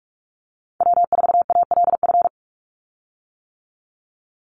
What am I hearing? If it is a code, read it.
Morse code, heard as a keyed tone, about 44 wpm, 700 Hz. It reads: W4ALF
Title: Call Signs - 45wpm